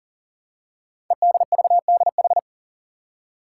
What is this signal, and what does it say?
Morse code, heard as a keyed tone, about 42 wpm, 690 Hz. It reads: EDVDH
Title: Code Groups - 5 Letters - 40wpm